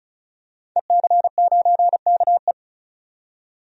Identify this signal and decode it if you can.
Morse code, heard as a keyed tone, about 34 wpm, 700 Hz. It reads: EC9KE
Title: DXCC-100 Call Signs - 5 Characters - 35wpm